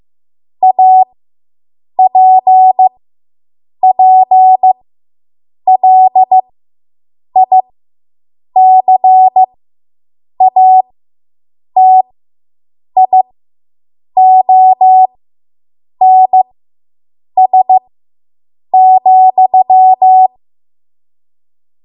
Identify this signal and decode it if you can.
Morse code, heard as a keyed tone, about 15 wpm, spaced as for 7 wpm, 750 Hz. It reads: APPLICATIONS,